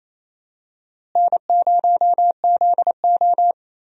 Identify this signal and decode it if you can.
Morse code, heard as a keyed tone, about 28 wpm, 700 Hz. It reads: N0ZO
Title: USA Call Signs - 4 Characters - Repeated 3x - 28wpm